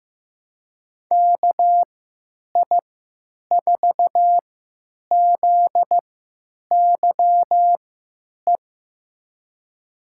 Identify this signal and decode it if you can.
Morse code, heard as a keyed tone, about 15 wpm, spaced as for 9 wpm, 700 Hz. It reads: KI4ZYE
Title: USA Call Signs - 3x Character Spacing - 15wpm